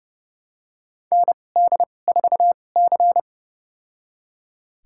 Morse code, heard as a keyed tone, about 29 wpm, spaced as for 22 wpm, 700 Hz. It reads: ND4C
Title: USA Call Signs - 4 Characters - 2x Character Spacing - 30wpm